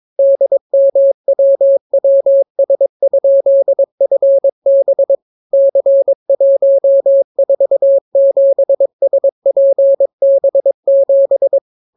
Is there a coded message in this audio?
Yes: Morse code from 0.2 to 11.6 s, heard as a keyed tone, about 22 wpm, 550 Hz. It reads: DMWWS?FB C147SPB7